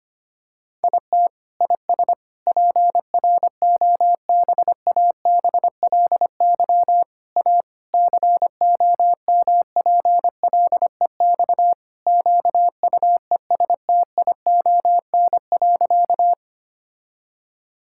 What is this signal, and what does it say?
Morse code, heard as a keyed tone, about 25 wpm, 700 Hz. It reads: IT IS PROBABLY A COMPLEX QUESTION.
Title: Sentences from Top 1000 Words - Review Part 6 - 25wpm